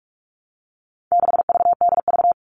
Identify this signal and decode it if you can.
Morse code, heard as a keyed tone, about 44 wpm, 700 Hz. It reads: 6VDV
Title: All Cut Numbers - 4-Digit 45wpm